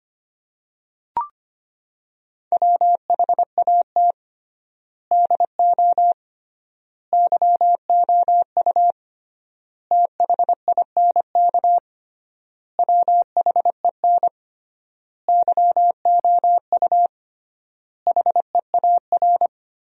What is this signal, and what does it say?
Morse code, heard as a keyed tone, about 25 wpm, 700 Hz. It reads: WHAT DO YOU THINK WHEN YOU HEAR
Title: 3x Word-Spacing - Sentences from Top 300 Words - 25wpm